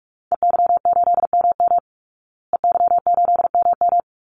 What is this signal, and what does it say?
Morse code, heard as a keyed tone, about 44 wpm, 700 Hz. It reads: EY8MM EY8MM
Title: Call Signs Repeated 2x - 45wpm